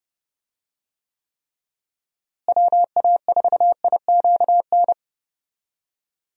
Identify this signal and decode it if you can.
Morse code, heard as a keyed tone, about 30 wpm, 700 Hz. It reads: WA4IQN
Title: USA Call Signs - 6 Characters - Repeated 3x - 30wpm